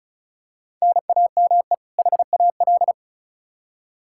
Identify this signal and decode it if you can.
Morse code, heard as a keyed tone, about 36 wpm, 700 Hz. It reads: NAME HAL
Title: Practice QSO - Exchange 1 - Beginning - 35wpm